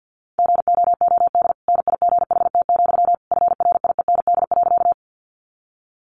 Morse code, heard as a keyed tone, about 50 wpm, 700 Hz. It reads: GOOD NIGHT, FRIEND.
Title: Sentences from Top 300 Words 50wpm